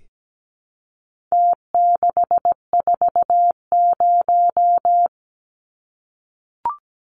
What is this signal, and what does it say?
Morse code, heard as a keyed tone, about 17 wpm, 700 Hz. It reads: T640
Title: Numbers - Contest Style - 4-Digit 17wpm